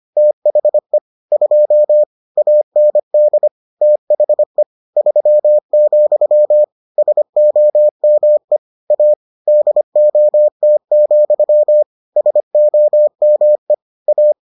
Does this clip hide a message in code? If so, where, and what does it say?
Yes: Morse code from 0.2 to 14.3 s, heard as a keyed tone, about 25 wpm, 600 Hz. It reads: THE 2 AND THE 3, SOME A DOT, SOME A